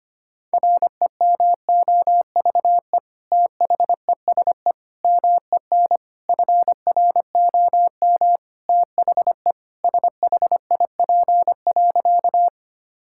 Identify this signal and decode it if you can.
Morse code, heard as a keyed tone, about 25 wpm, 700 Hz. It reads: REMOVE THESE MEN FROM THE SHIP.
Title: Sentences from Top 1000 Words - Review Part 5 - 25wpm